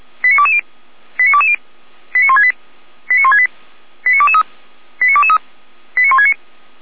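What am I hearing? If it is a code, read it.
Five-tone selective call (ZVEI), heard as quick runs of five tones, tones of about 70 ms. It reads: 89220 89220 89168 89168 89223 89223 89169
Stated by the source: ZVEI five-tone selective call received by SDR, archived by feuerwehr-orlamuende.org fire brigade.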